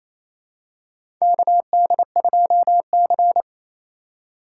Morse code, heard as a keyed tone, about 27 wpm, 700 Hz. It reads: KD2C
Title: USA Call Signs - 4 Characters - 6x Character Spacing - 28wpm